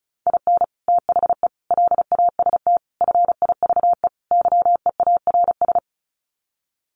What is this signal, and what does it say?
Morse code, heard as a keyed tone, about 34 wpm, 700 Hz. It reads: IN THE LAST FIVE YEARS
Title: Sets of 5 Words Encore 35wpm